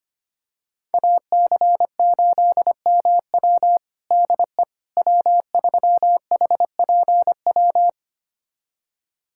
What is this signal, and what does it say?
Morse code, heard as a keyed tone, about 25 wpm, 700 Hz. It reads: AC8MW DE W3HPW
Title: USA Callsign DE Callsign - 5 Characters - 25wpm